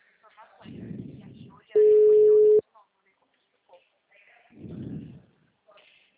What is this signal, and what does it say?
Ringback tone (Europe, 425 Hz), with speech.